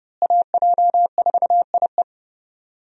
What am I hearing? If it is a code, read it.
Morse code, heard as a keyed tone, about 30 wpm, 700 Hz. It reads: AJ4IE
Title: USA Call Signs - 5 Characters - Repeated 3x - 30wpm